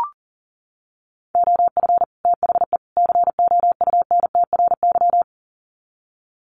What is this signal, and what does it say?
Morse code, heard as a keyed tone, about 40 wpm, 700 Hz. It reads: OF THE COUNTRY
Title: Sets of 3 Words 40wpm